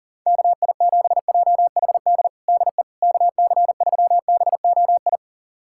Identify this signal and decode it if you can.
Morse code, heard as a keyed tone, about 40 wpm, 700 Hz. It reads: KI7JHD DE KC3BOI
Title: USA Callsign DE Callsign 40wpm